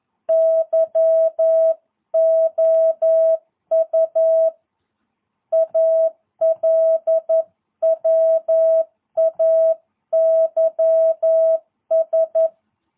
Morse code, heard as a keyed tone, about 11 wpm, 650 Hz. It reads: YOU ALWAYS
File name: jh8cbh / 2022cw